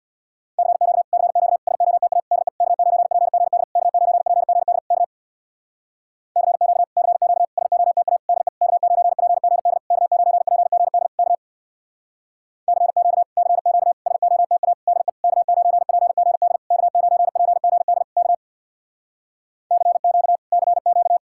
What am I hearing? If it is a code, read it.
Morse code, heard as a keyed tone, about 62 wpm, 700 Hz. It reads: CQ CQ SOTA DE K9PGD K9PGD K CQ CQ SOTA DE K9PGD K9PGD K CQ CQ SOTA DE K9PGD K9PGD K CQ CQ
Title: Speed-Racing Common QSO Phrases - 50wpm